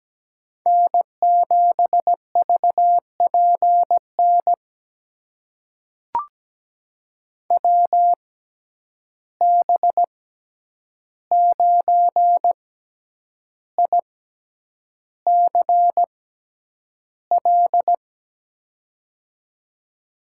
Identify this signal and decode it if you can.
Morse code, heard as a keyed tone, about 17 wpm, spaced as for 8 wpm, 700 Hz. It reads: N7VPN WB9ICL
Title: USA Call Signs - 6x Character Spacing - 17wpm